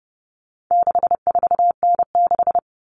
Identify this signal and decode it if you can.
Morse code, heard as a keyed tone, about 30 wpm, 700 Hz. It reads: 64N6
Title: All Cut Numbers - 4-Digit 30wpm